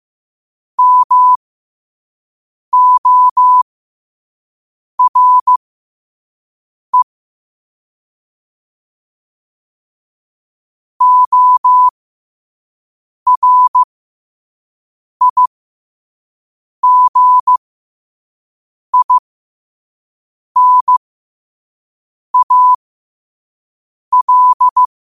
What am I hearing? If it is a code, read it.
Morse code, heard as a keyed tone, about 15 wpm, spaced as for 5 wpm, 1000 Hz. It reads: MORE ORIGINAL
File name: qotdfiles / 5wpm